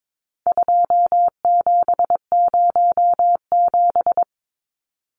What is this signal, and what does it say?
Morse code, heard as a keyed tone, about 22 wpm, 700 Hz. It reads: 2707
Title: Numbers - 4-Digit 22wpm